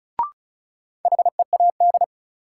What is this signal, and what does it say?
Morse code, heard as a keyed tone, about 36 wpm, 700 Hz. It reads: HEAD
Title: Sets of 3 Words - New Word Drill - 35wpm